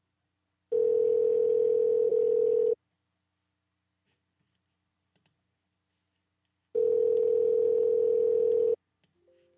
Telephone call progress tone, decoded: ringback tone